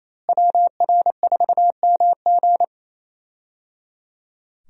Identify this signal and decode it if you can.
Morse code, heard as a keyed tone, about 28 wpm, 700 Hz. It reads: WR4MG
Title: USA Call Signs - 5 Characters - Repeated 3x - 28wpm